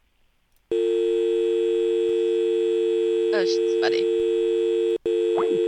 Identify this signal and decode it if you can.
Dial tone (UK), 350 + 450 Hz, with speech.